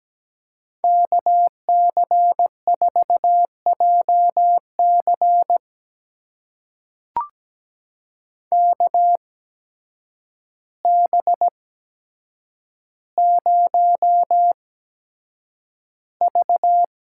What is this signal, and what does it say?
Morse code, heard as a keyed tone, about 17 wpm, spaced as for 8 wpm, 700 Hz. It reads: KC4JC KB0V
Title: USA Call Signs - 5 Characters - 8x Character Spacing - 17wpm